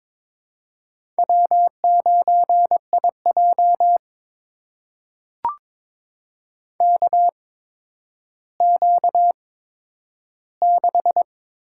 Morse code, heard as a keyed tone, about 21 wpm, spaced as for 11 wpm, 700 Hz. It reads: W9IJ KQ6
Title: USA Call Signs - 4 Characters - 8x Character Spacing - 22wpm